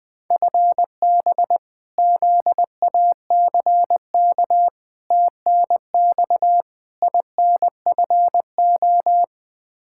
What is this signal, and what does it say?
Morse code, heard as a keyed tone, about 20 wpm, 700 Hz. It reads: FB ZACK TNX INFO
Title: Practice QSO - Exchange 3 - More Info - 20wpm